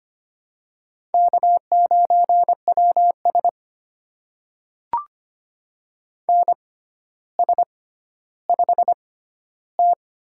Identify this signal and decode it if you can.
Morse code, heard as a keyed tone, about 25 wpm, spaced as for 13 wpm, 700 Hz. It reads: K9WS NS5T
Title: USA Call Signs - 4 Characters - 6x Character Spacing - 25wpm